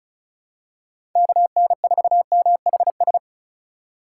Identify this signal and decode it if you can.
Morse code, heard as a keyed tone, about 36 wpm, 700 Hz. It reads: KN4MHS
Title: USA Call Signs - 2x Character Spacing - 35wpm